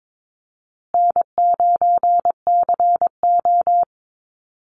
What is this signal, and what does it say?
Morse code, heard as a keyed tone, about 22 wpm, 700 Hz. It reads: N9CO
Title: Call Signs Repeated 2x - 22wpm